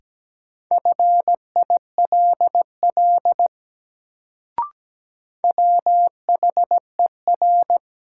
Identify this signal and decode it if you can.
Morse code, heard as a keyed tone, about 17 wpm, 700 Hz. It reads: FILL WHER
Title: Sentences from Top 1500 Words - New Word Drill Alt - 17wpm